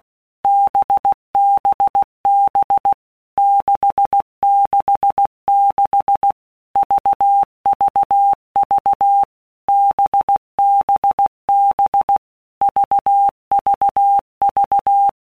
Morse code, heard as a keyed tone, about 16 wpm, 800 Hz. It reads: BBB 666 VVV BBB VVV
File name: Basic Letter Groups By WPM's / Groups@15 / VB46@15